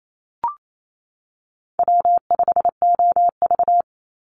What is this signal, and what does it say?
Morse code, heard as a keyed tone, about 28 wpm, 700 Hz. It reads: W5OV
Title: Easy Call Signs - 28wpm